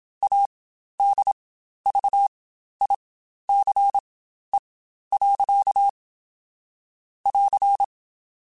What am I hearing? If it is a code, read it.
Morse code, heard as a keyed tone, about 27 wpm, spaced as for 12 wpm, 800 Hz. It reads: ADVICE. <AR>
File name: Quotes Spaced / Quotes Spaced-mp3s / Advice sp @25